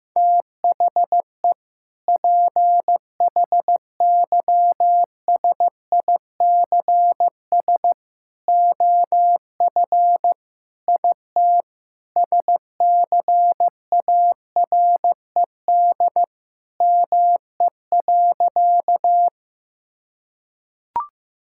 Morse code, heard as a keyed tone, about 15 wpm, 700 Hz. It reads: THE PHYSICS OF IT SCARED ME.
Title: Sentences from Top 1000 Words - Review Part 5 - 15wpm